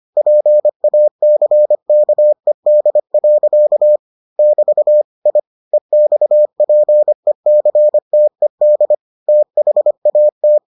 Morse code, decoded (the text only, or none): PACKED. <BT> I EXPECTED THAT